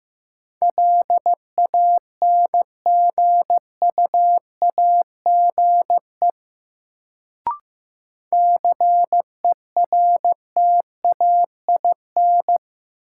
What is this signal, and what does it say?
Morse code, heard as a keyed tone, about 15 wpm, 700 Hz. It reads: LANGUAGE CERTAIN
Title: Sentences from Top 500 Words - New Word Drill - 15wpm